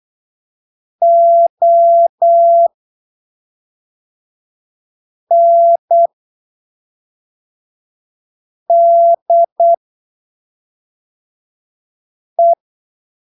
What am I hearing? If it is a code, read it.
Morse code, heard as a keyed tone, about 8 wpm, spaced as for 3 wpm, 680 Hz. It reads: ONDE
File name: arquivos / texto / texto 13 08ppm